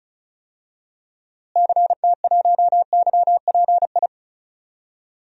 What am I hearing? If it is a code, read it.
Morse code, heard as a keyed tone, about 36 wpm, 700 Hz. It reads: CT1YPI
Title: DXCC-100 Call Signs - 6 Characters - Repeated 3x - 35wpm